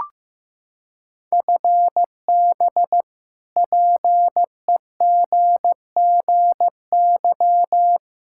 Morse code, heard as a keyed tone, about 15 wpm, 700 Hz. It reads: FB PEGGY
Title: Practice QSO - Exchange 3 - More Info - 15wpm